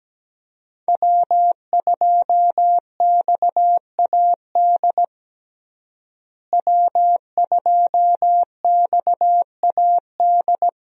Morse code, heard as a keyed tone, about 17 wpm, 700 Hz. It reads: W2XAD W2XAD
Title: USA Call Signs - 5 Characters - Repeated 3x - 17wpm